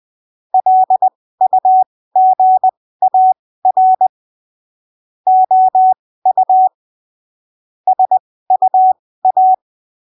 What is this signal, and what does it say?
Morse code, heard as a keyed tone, about 20 wpm, spaced as for 15 wpm, 750 Hz. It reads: LUGAR OU SUA
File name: arquivos / texto / texto 28 20ppm